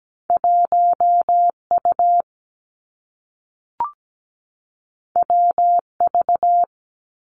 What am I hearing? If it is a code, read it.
Morse code, heard as a keyed tone, about 17 wpm, 700 Hz. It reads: 1 U W V
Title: Two Challenging Letter-Number Combinations 17wpm